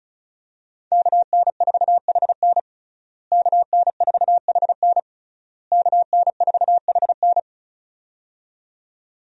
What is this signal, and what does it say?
Morse code, heard as a keyed tone, about 34 wpm, 700 Hz. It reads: KN4HN KN4HN KN4HN
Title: USA Call Signs - 5 Characters - Repeated 3x - 35wpm